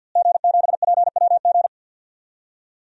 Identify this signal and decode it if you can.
Morse code, heard as a keyed tone, about 52 wpm, 700 Hz. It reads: G7PWG
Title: DXCC-100 Call Signs - 5 Characters - Repeated 3x - 50wpm